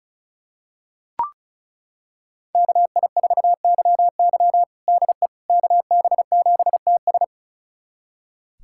Morse code, heard as a keyed tone, about 36 wpm, 700 Hz. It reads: KI4YY DE KB7TS
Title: USA Callsign DE Callsign - 5 Characters - 35wpm